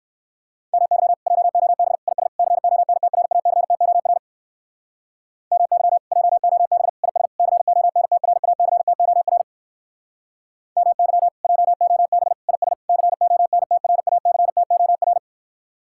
Morse code, heard as a keyed tone, about 58 wpm, 700 Hz. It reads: MY JOB IS CONTRACTOR MY JOB IS CONTRACTOR MY JOB IS CONTRACTOR